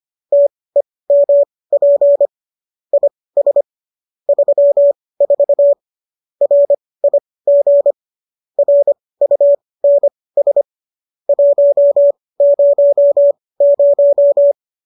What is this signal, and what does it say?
Morse code, heard as a keyed tone, about 25 wpm, spaced as for 18 wpm, 570 Hz. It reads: TEMP IS 34 RIG RUNS 100